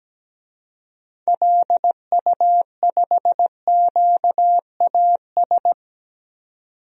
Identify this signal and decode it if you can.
Morse code, heard as a keyed tone, about 17 wpm, 700 Hz. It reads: LU5QAS
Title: DXCC-100 Call Signs - 6 Characters - 17wpm